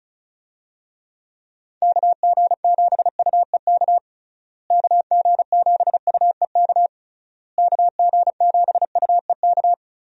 Morse code, heard as a keyed tone, about 36 wpm, 700 Hz. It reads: KG7UEK KG7UEK KG7UEK
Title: USA Call Signs - 6 Characters - Repeated 3x - 35wpm